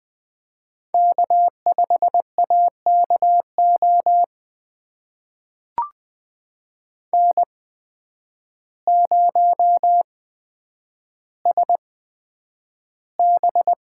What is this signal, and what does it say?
Morse code, heard as a keyed tone, about 20 wpm, spaced as for 9 wpm, 700 Hz. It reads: K5AKO N0SB
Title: USA Call Signs - 8x Character Spacing - 20wpm